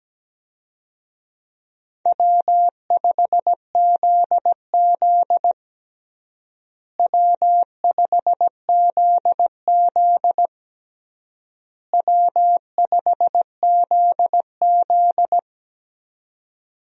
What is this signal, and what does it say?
Morse code, heard as a keyed tone, about 17 wpm, 700 Hz. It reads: W5ZZ W5ZZ W5ZZ
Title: USA Call Signs - 4 Characters - Repeated 3x - 17wpm